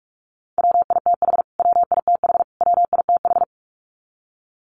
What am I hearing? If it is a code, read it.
Morse code, heard as a keyed tone, about 44 wpm, 700 Hz. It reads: WITH WITH WITH
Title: Top 200 Words Repeated 45wpm